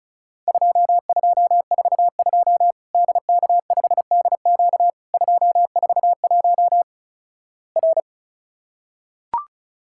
Morse code, heard as a keyed tone, about 36 wpm, 700 Hz. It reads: 2242 DK5DQ 241 R
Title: Worked All Europe DX Contest - QTC Exchange - 35wpm